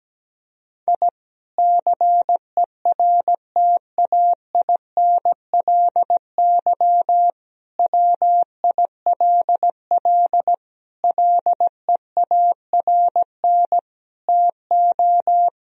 Morse code, heard as a keyed tone, about 17 wpm, 700 Hz. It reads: I CERTAINLY WILL LEARN TO
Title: Sentences from Top 1000 Words - Review Part 4 - 17wpm